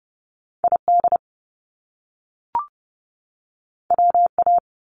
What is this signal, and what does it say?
Morse code, heard as a keyed tone, about 30 wpm, 700 Hz. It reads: ID WA
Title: US States 30wpm